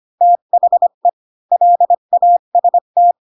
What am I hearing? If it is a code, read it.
Morse code, heard as a keyed tone, about 25 wpm, 700 Hz. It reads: THE LAST
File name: Pinocchio / Pinocchio ch3 20 25 ik1hge